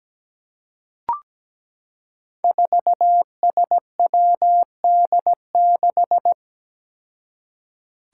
Morse code, heard as a keyed tone, about 17 wpm, 700 Hz. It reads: 4SWD6
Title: Five Challenging Letter-Number Combinations 17wpm